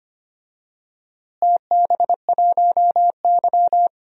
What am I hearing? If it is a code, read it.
Morse code, heard as a keyed tone, about 25 wpm, 700 Hz. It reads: TB1Y
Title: DXCC-100 Call Signs - 4 Characters - Repeated 3x - 25wpm